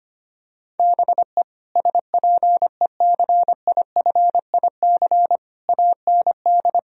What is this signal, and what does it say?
Morse code, heard as a keyed tone, about 25 wpm, 700 Hz. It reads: BE SPECIFIC AND
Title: Sentences from Top 1000 Words - Review Part 3 - 25wpm